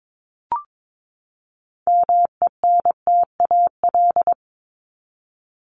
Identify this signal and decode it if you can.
Morse code, heard as a keyed tone, about 22 wpm, 700 Hz. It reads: MENTAL
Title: Top 501-750 Words 22wpm